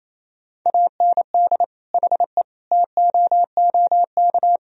Morse code, heard as a keyed tone, about 28 wpm, 700 Hz. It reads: AND HE TOOK
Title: Sentences from Top 1000 Words - Review Part 4 - 28wpm